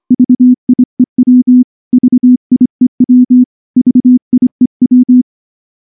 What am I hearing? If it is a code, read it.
Morse code, heard as a keyed tone, about 24 wpm, 250 Hz. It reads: VIEW VIEW VIEW
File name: morse / morse-training-25WPM